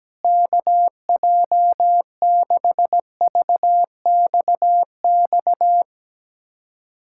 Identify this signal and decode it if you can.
Morse code, heard as a keyed tone, about 17 wpm, 700 Hz. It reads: KJ6VXX
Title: USA Callsign DE Callsign - 6 Characters - 17wpm